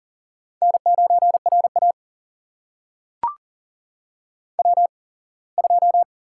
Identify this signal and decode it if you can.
Morse code, heard as a keyed tone, about 40 wpm, spaced as for 24 wpm, 700 Hz. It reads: N9RA W2
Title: USA Call Signs - 8x Character Spacing - 40wpm